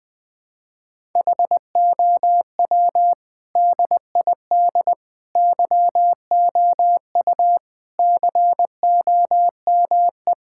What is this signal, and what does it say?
Morse code, heard as a keyed tone, about 20 wpm, 700 Hz. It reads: HOW DID YOU COME
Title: Sentences with 8-Word Phrases - Spoken Beginning - 20wpm